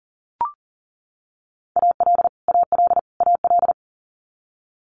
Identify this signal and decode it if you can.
Morse code, heard as a keyed tone, about 38 wpm, 700 Hz. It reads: AL AL AL
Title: Multi-letter Phonemes Repeated 3x 40wpm